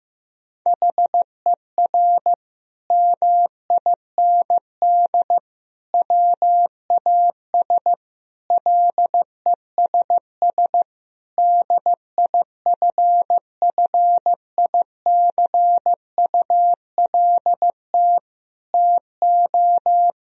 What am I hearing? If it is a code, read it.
Morse code, heard as a keyed tone, about 16 wpm, 700 Hz. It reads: HER MIND WAS LESS DIFFICULT TO
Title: Sentences from Top 1000 Words - Review Part 2 - 15wpm